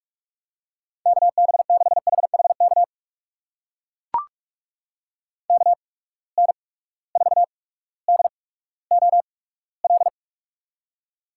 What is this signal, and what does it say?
Morse code, heard as a keyed tone, about 44 wpm, spaced as for 19 wpm, 700 Hz. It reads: KB6HHK KN4DOL
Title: USA Call Signs - 8x Character Spacing - 45wpm